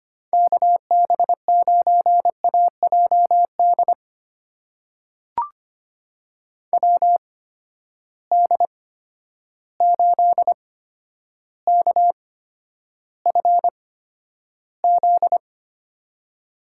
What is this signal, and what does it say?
Morse code, heard as a keyed tone, about 25 wpm, spaced as for 11 wpm, 700 Hz. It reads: KB9AJD WD8KFZ
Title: USA Call Signs - 6 Characters - 8x Character Spacing - 25wpm